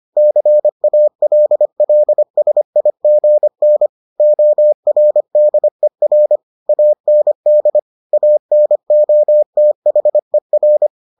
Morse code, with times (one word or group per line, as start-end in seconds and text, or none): 0.2-3.9 CALLSIGN
4.2-6.4 ORDER
6.7-7.8 AND
8.1-10.9 ANOTHER